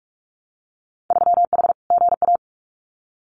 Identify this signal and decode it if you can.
Morse code, heard as a keyed tone, about 44 wpm, 700 Hz. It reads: 3H GA
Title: Winter Field Day Exchanges Encore 45wpm